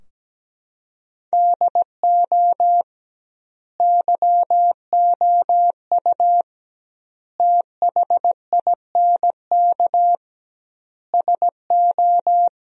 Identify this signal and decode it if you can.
Morse code, heard as a keyed tone, about 17 wpm, 700 Hz. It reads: DO YOU THINK SO
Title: 2x Word-Spacing - Sentences from Top 200 Words - 17wpm